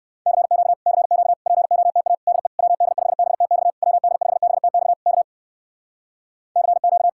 Morse code, heard as a keyed tone, about 59 wpm, 700 Hz. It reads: CQ CQ POTA DE WD5BTX WD5BTX K CQ